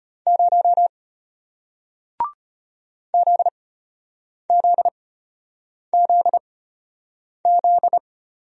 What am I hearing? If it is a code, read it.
Morse code, heard as a keyed tone, about 33 wpm, spaced as for 8 wpm, 690 Hz. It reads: 0 ZZZZ